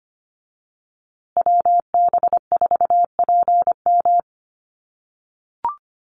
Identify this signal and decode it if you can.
Morse code, heard as a keyed tone, about 25 wpm, 700 Hz. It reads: WB4PM
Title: USA Call Signs Repeated 2x 25wpm